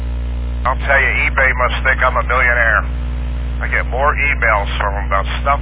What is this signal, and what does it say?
Emergency Alert System (SAME) tones, heard as harsh buzzing bursts, with speech. It is heard from 0.8 to 4.5 s.